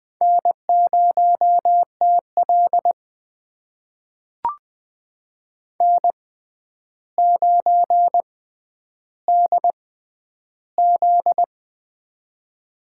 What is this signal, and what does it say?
Morse code, heard as a keyed tone, about 20 wpm, spaced as for 10 wpm, 700 Hz. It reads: N0TL N9DZ
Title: USA Call Signs - 4 Characters - 6x Character Spacing - 20wpm